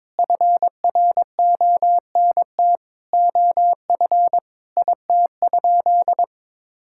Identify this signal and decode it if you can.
Morse code, heard as a keyed tone, about 22 wpm, 700 Hz. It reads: FRONT OF IT?
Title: Sentences from Top 1000 Words - Review Part 5 - 22wpm